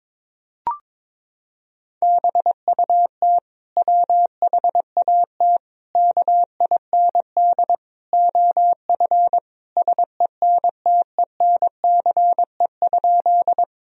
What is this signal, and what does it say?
Morse code, heard as a keyed tone, about 22 wpm, 700 Hz. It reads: BUT WHAT KIND OF SENTENCE?
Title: Sentences from Top 1000 Words - Review Part 1 - 22wpm